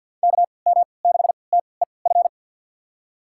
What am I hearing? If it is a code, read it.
Morse code, heard as a keyed tone, about 46 wpm, spaced as for 27 wpm, 700 Hz. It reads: KM6TEF
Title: USA Call Signs - 3x Character Spacing - 50wpm